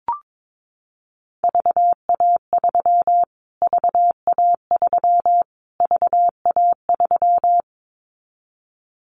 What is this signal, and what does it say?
Morse code, heard as a keyed tone, about 22 wpm, 700 Hz. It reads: VA3 VA3 VA3
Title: Callsign Prefixes for Top 25 DXCC - Repeated 3x - 22wpm